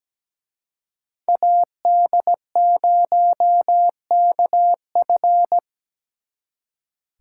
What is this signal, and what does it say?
Morse code, heard as a keyed tone, about 20 wpm, 700 Hz. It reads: AD0KF
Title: Speed Racing - USA 2x2 Call Signs 17wpm